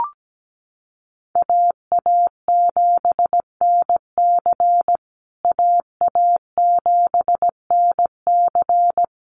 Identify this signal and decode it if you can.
Morse code, heard as a keyed tone, about 17 wpm, 700 Hz. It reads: AA7NC AA7NC
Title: USA Call Signs Repeated 3x 17wpm